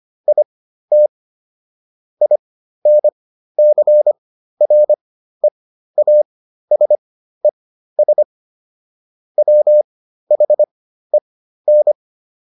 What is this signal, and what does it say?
Morse code, heard as a keyed tone, about 25 wpm, spaced as for 13 wpm, 600 Hz. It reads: IT INCREASES WHEN